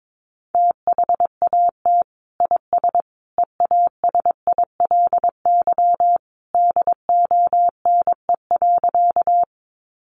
Morse code, heard as a keyed tone, about 22 wpm, 700 Hz. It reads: THAT IS EASILY DONE.
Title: Sentences from Top 900 Words 22wpm